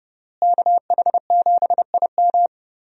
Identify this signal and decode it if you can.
Morse code, heard as a keyed tone, about 30 wpm, 700 Hz. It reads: KH7IM
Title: USA Call Signs - 5 Characters - Repeated 3x - 30wpm